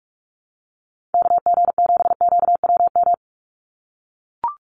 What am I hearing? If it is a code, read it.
Morse code, heard as a keyed tone, about 44 wpm, 700 Hz. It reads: KG7QWM
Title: USA Call Signs 45wpm